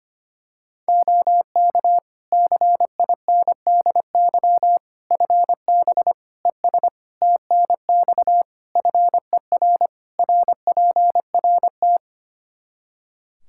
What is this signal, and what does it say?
Morse code, heard as a keyed tone, about 25 wpm, 700 Hz. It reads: OK CINDY FB ES TNX FER RPRT
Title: K1USN SST QSO Practice 25wpm